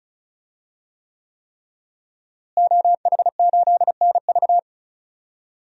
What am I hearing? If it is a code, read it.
Morse code, heard as a keyed tone, about 36 wpm, 700 Hz. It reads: OH8NV
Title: DXCC-100 Call Signs - 5 Characters - Repeated 3x - 35wpm